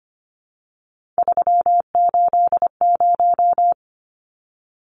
Morse code, heard as a keyed tone, about 25 wpm, 690 Hz. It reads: 380
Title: Numbers - 3-Digit 25wpm